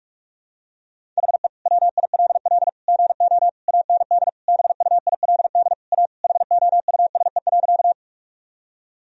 Morse code, heard as a keyed tone, about 47 wpm, 700 Hz. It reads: HE WILL GO AND BUILD A HOUSE.
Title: Sentences from Top 1000 Words - Review Part 1 - 45wpm